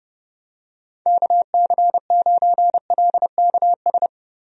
Morse code, heard as a keyed tone, about 30 wpm, 700 Hz. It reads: KC9LKS
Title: USA Call Signs - 6 Characters - Repeated 3x - 30wpm